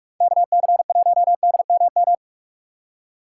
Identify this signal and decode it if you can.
Morse code, heard as a keyed tone, about 46 wpm, 700 Hz. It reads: KC1DMM
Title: USA Call Signs - 6 Characters - 6x Character Spacing - 45wpm